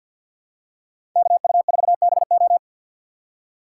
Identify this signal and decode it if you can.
Morse code, heard as a keyed tone, about 47 wpm, 700 Hz. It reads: KU4BO
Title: USA Call Signs - 5 Characters - 6x Character Spacing - 50wpm